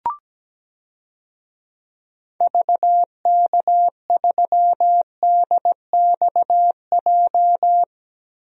Morse code, heard as a keyed tone, about 17 wpm, 700 Hz. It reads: VK3DXJ
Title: DXCC-100 Call Signs - 6 Characters - 17wpm